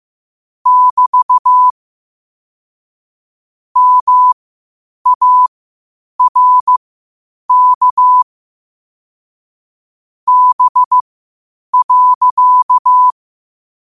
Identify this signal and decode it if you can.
Morse code, heard as a keyed tone, about 15 wpm, spaced as for 8 wpm, 1000 Hz. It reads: <BT> MARK B.
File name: qotdfiles / 8wpm